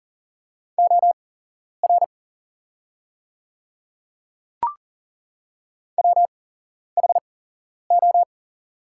Morse code, heard as a keyed tone, about 40 wpm, spaced as for 11 wpm, 690 Hz. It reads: OR WHO
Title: 8x Character Spacing - Top 100 Words - 40wpm